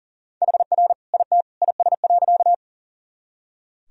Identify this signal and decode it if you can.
Morse code, heard as a keyed tone, about 37 wpm, 700 Hz. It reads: HR IT IS.
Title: Sentences from Top 200 Words and Abbreviations 40wpm